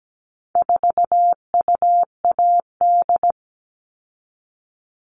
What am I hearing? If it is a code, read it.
Morse code, heard as a keyed tone, about 17 wpm, 700 Hz. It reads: 4UAD
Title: All Cut Numbers - 4-Digit 17wpm